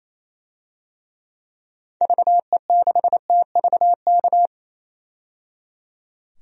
Morse code, heard as a keyed tone, about 28 wpm, 700 Hz. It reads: VE6TVK
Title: DXCC-100 Call Signs - 6 Characters - 28wpm